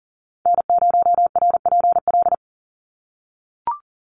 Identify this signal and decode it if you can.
Morse code, heard as a keyed tone, about 40 wpm, 700 Hz. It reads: N0RPL
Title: USA Call Signs 40wpm